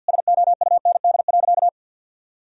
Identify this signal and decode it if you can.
Morse code, heard as a keyed tone, about 52 wpm, 700 Hz. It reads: SOUND.
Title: Sentences from Top 1000 Words - Review Part 1 - 50wpm